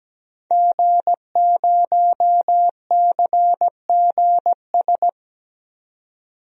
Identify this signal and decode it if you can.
Morse code, heard as a keyed tone, about 17 wpm, 700 Hz. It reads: G0CGS
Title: DXCC-100 Call Signs - 5 Characters - 17wpm